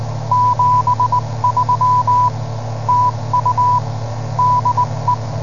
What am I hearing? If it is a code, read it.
Morse code, heard as a keyed tone, about 20 wpm, 990 Hz, with noise. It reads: 73 TU DE